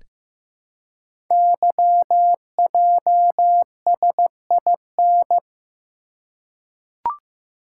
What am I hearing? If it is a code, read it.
Morse code, heard as a keyed tone, about 15 wpm, 690 Hz. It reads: YJSIN
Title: Code Groups - 5 Letters - 15wpm